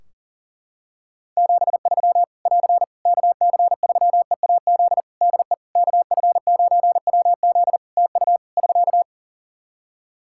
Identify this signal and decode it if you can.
Morse code, heard as a keyed tone, about 37 wpm, 700 Hz. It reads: 73 <AR> KC3EAZ DE KF9WZ TU <SK>
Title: Practice QSO - Exchange 4 - Ending - 40wpm